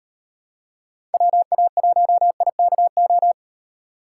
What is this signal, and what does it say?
Morse code, heard as a keyed tone, about 33 wpm, 700 Hz. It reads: WA1IKO